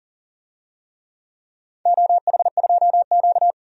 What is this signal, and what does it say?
Morse code, heard as a keyed tone, about 40 wpm, 700 Hz. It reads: OH2Q
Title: DXCC-100 Call Signs - 4 Characters - 40wpm